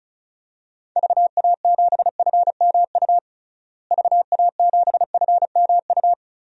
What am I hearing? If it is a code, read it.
Morse code, heard as a keyed tone, about 34 wpm, 700 Hz. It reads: VA7FMU VA7FMU
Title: DXCC-100 Call Signs - 6 Characters - Repeated 2x - 35wpm